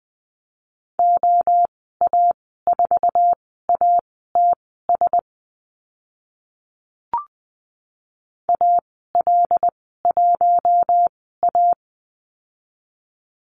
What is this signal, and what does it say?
Morse code, heard as a keyed tone, about 20 wpm, spaced as for 14 wpm, 700 Hz. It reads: OA4ATS AL1A
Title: 088 Lesson 15g - Number 5 - Call Signs 20wpm